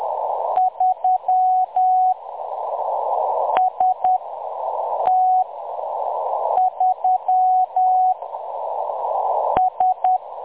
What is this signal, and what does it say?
Morse code, heard as a keyed tone, about 10 wpm, spaced as for 6 wpm, 740 Hz. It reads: 3 ST3 S
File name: M23-ST3- -0858utc-10184khz